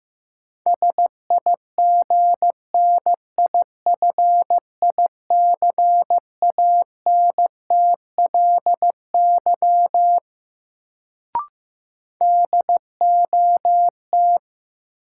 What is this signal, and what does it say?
Morse code, heard as a keyed tone, about 15 wpm, 700 Hz. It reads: SIGNIFICANTLY DOT
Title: Sentences from Top 1500 Words - New Word Drill - 15wpm